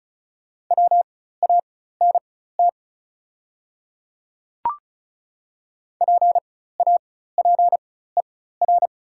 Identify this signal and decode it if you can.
Morse code, heard as a keyed tone, about 35 wpm, spaced as for 16 wpm, 700 Hz. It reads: WANT PAPER
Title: 4x Character Spacing - Top 300 Words - 35wpm